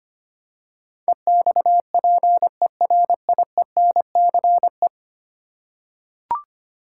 Morse code, heard as a keyed tone, about 25 wpm, 700 Hz. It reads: EXPERIENCE
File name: practice / 25wpm / Ten Letter Words - 25wpm